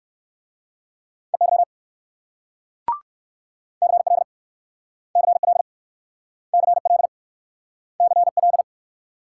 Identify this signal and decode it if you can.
Morse code, heard as a keyed tone, about 61 wpm, 690 Hz. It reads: EY CL CL CL CL